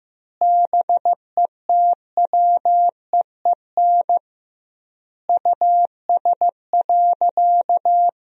Morse code, heard as a keyed tone, about 15 wpm, 700 Hz. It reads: BETWEEN US.
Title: 2x Word-Spacing - Sentences from Top 500 Words - 15wpm